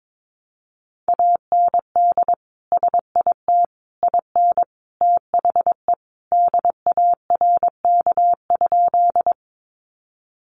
Morse code, heard as a keyed tone, about 22 wpm, 700 Hz. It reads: AND SIT IN THE DARK?
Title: Sentences from Top 500 Words 22wpm